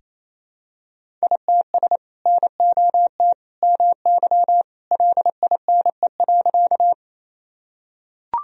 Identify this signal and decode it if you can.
Morse code, heard as a keyed tone, about 28 wpm, 700 Hz. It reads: ITS NOT MY LINE.
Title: Sentences from Top 1000 Words - Review Part 1 - 28wpm